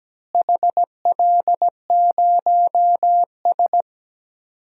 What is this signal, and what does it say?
Morse code, heard as a keyed tone, about 17 wpm, 700 Hz. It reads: HL0S
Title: DXCC-100 Call Signs - 4 Characters - Repeated 3x - 17wpm